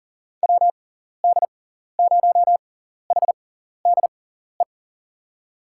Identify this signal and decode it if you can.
Morse code, heard as a keyed tone, about 37 wpm, spaced as for 14 wpm, 700 Hz. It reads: WD0HDE
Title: USA Call Signs - 6 Characters - 6x Character Spacing - 40wpm